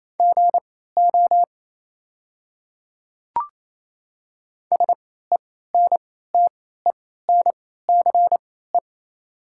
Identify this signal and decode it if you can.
Morse code, heard as a keyed tone, about 28 wpm, spaced as for 16 wpm, 700 Hz. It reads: GO SENTENCE
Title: 3x Character Spacing - Top 200 Words - 28wpm